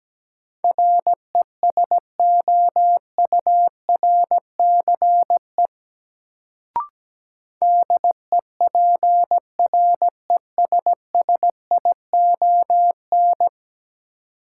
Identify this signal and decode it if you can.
Morse code, heard as a keyed tone, about 17 wpm, 700 Hz. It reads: RESOURCE DEPRESSION
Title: Sentences from Top 1300 Words - New Word Drill - 17wpm